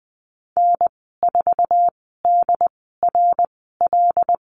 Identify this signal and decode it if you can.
Morse code, heard as a keyed tone, about 20 wpm, spaced as for 15 wpm, 700 Hz. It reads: N4DRL
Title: 082 Lesson 14g - Number 2 - Call Signs 20wpm